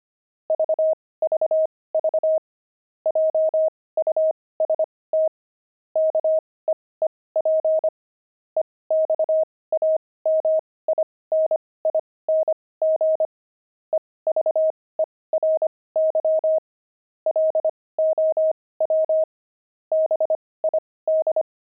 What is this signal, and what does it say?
Morse code, heard as a keyed tone, about 25 wpm, spaced as for 18 wpm, 630 Hz. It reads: VVV JUST KEEP EXAMINING EVERY LOW BID
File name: pangram-fs-18-cs-25-low